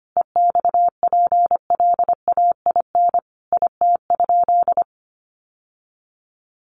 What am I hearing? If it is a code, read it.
Morse code, heard as a keyed tone, about 25 wpm, 700 Hz. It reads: EXPLAIN IT?
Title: Sentences from Top 1100 Words 25wpm